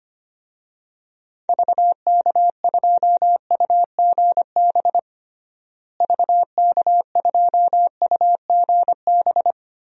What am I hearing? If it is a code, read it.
Morse code, heard as a keyed tone, about 25 wpm, 700 Hz. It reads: VK2UGB VK2UGB
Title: DXCC-100 Call Signs - 6 Characters - Repeated 3x - 25wpm